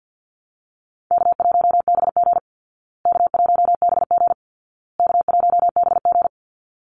Morse code, heard as a keyed tone, about 48 wpm, 700 Hz. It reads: K1BG K1BG K1BG
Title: Easy Call Signs Repeated 3x - 50wpm